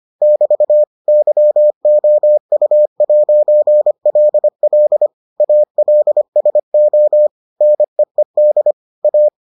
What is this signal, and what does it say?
Morse code, heard as a keyed tone, about 25 wpm, 600 Hz. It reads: <BT> YOU'LL ALSO NEED A